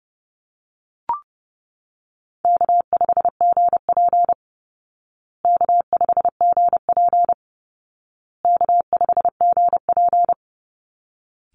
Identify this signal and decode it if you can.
Morse code, heard as a keyed tone, about 30 wpm, 700 Hz. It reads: K5GP K5GP K5GP
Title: Easy Call Signs Repeated 3x - 30wpm